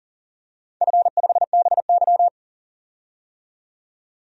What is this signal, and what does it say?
Morse code, heard as a keyed tone, about 40 wpm, 700 Hz. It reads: F5BY
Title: DXCC-100 Call Signs - 4 Characters - 40wpm